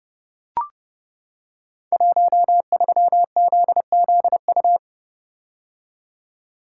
Morse code, heard as a keyed tone, about 28 wpm, 700 Hz. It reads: 13ZZU
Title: Code Groups - 5 Letter:Numbers - 30wpm